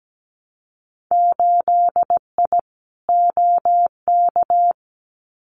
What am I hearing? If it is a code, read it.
Morse code, heard as a keyed tone, about 17 wpm, 700 Hz. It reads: 8I OK
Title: Winter Field Day Exchanges Encore 17wpm